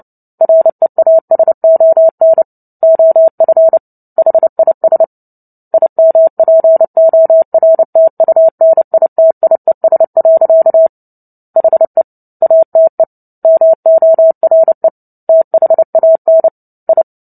Morse code, heard as a keyed tone, about 29 wpm, 650 Hz. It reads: REASON OF HIS IMPORTUNITIES. HE ATE MORE THAN I
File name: Book Two / 03-Chapter 3 - 29 WPM